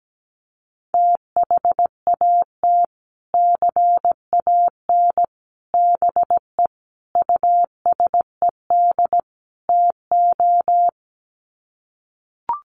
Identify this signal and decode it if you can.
Morse code, heard as a keyed tone, about 17 wpm, 700 Hz. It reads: THAT CAN BE USED TO
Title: Sets of 5 Words Encore 17wpm